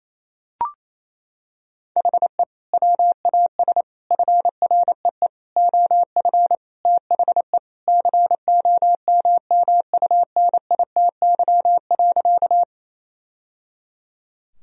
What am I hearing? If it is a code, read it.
Morse code, heard as a keyed tone, about 28 wpm, 700 Hz. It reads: HE WAS FREE OF THE COMMUNITY.
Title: Sentences from Top 1000 Words - Review Part 4 - 28wpm